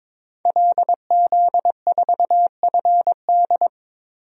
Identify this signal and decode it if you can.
Morse code, heard as a keyed tone, about 21 wpm, 700 Hz. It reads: LZ4FD
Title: DXCC-100 Call Signs - 5 Characters - Repeated 3x - 22wpm